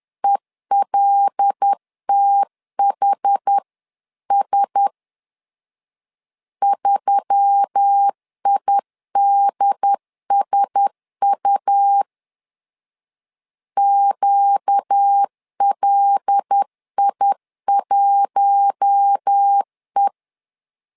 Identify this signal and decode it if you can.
Morse code, heard as a keyed tone, about 10 wpm, 800 Hz. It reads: ELTHS 3IDSU QLI1E
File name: LEZIONI / 5 LBVC / 5 L 10wpm 2259